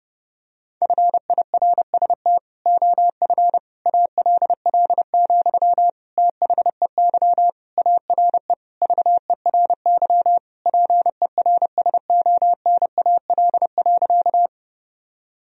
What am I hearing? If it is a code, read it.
Morse code, heard as a keyed tone, about 30 wpm, 700 Hz. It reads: FIRST OF ALL, THEY ARE VERY PERSONAL.
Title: Sentences from Top 1000 Words - Review Part 4 - 30wpm